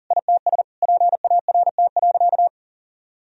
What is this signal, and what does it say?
Morse code, heard as a keyed tone, about 37 wpm, 700 Hz. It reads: ITS PART.
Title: Sentences from Top 1000 Words - Review Part 3 - 40wpm